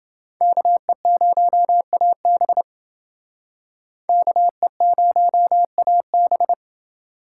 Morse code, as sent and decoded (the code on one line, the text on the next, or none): -.- . ----- .- -... / -.- . ----- .- -...
KE0AB KE0AB